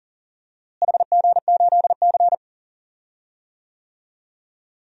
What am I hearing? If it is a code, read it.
Morse code, heard as a keyed tone, about 37 wpm, 700 Hz. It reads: HG8C
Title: DXCC-100 Call Signs - 4 Characters - Repeated 3x - 40wpm